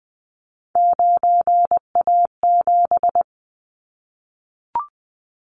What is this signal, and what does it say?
Morse code, heard as a keyed tone, about 19 wpm, 700 Hz. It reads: 9A7
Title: Callsign Prefixes for Top 25 DXCC - Repeated 3x - 20wpm